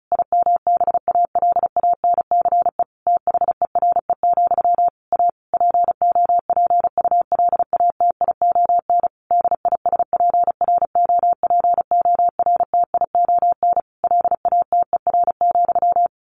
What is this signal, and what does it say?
Morse code, heard as a keyed tone, about 34 wpm, 700 Hz. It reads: IMBALANCE THERE, A POPULATION DISPROPORTION LATER,
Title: A Sound Of Thunder 35wpm